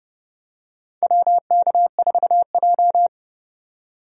Morse code, heard as a keyed tone, about 30 wpm, 700 Hz. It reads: WK4J
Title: USA Call Signs - 4 Characters - 30wpm